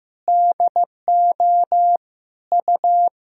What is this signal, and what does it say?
Morse code, heard as a keyed tone, about 15 wpm, 700 Hz. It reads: DO U
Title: Sentences from Top 300 Words and Abbreviations 15wpm